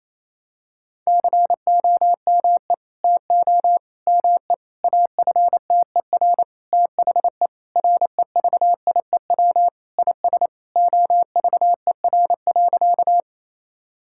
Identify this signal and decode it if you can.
Morse code, heard as a keyed tone, about 28 wpm, 700 Hz. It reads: COME TO ME AFTER THE REVIEW IS OVER.
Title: Sentences from Top 1000 Words - Review Part 6 - 28wpm